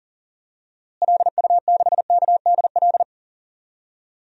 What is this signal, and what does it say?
Morse code, heard as a keyed tone, about 40 wpm, 700 Hz. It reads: LU6KDL
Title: DXCC-100 Call Signs - 6 Characters - Repeated 3x - 40wpm